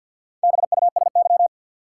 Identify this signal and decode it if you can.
Morse code, heard as a keyed tone, about 47 wpm, 700 Hz. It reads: BUSY
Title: Four Letter Words - 50wpm